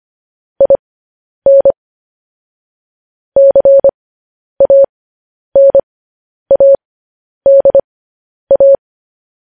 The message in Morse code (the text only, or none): IN CANADA